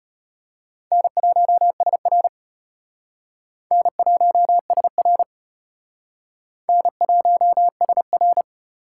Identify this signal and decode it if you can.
Morse code, heard as a keyed tone, about 31 wpm, 700 Hz. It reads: N1SR N1SR N1SR
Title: Speed Racing - USA 1x2 Call Signs 25wpm